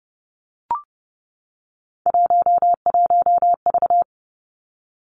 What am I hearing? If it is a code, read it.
Morse code, heard as a keyed tone, about 32 wpm, 700 Hz. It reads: 11V
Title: Three Challenging Letter-Number Combinations 30wpm